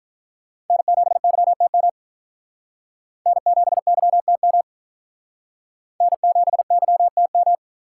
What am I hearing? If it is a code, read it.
Morse code, heard as a keyed tone, about 43 wpm, 700 Hz. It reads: N7YTM N7YTM N7YTM
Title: Speed Racing - USA 1x3 Call Signs 35wpm